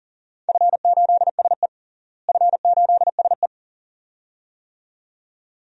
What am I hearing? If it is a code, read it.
Morse code, heard as a keyed tone, about 40 wpm, 700 Hz. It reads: F8SE F8SE
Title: DXCC-100 Call Signs - 4 Characters - Repeated 2x - 40wpm